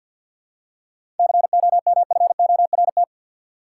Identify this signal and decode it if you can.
Morse code, heard as a keyed tone, about 48 wpm, 700 Hz. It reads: COMFORT
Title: Seven Letter Words - 50wpm